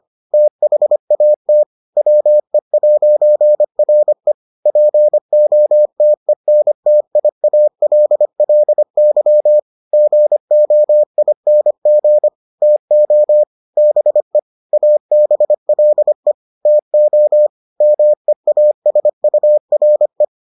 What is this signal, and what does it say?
Morse code, heard as a keyed tone, about 25 wpm, 600 Hz. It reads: THAT WE'RE POTENTIALLY GOING TO BE ABLE TO MEASURE